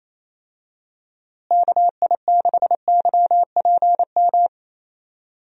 Morse code, heard as a keyed tone, about 28 wpm, 700 Hz. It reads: KI6YPM
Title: USA Call Signs - 6 Characters - Repeated 3x - 28wpm